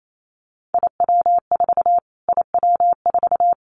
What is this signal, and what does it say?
Morse code, heard as a keyed tone, about 28 wpm, 700 Hz. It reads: IW4 IW4
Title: Callsign Prefixes for Top 25 DXCC - Repeated 3x - 28wpm